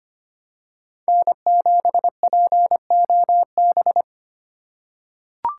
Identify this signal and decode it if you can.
Morse code, heard as a keyed tone, about 25 wpm, 700 Hz. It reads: N7POB
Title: USA Call Signs - 5 Characters - 2x Character Spacing - 25wpm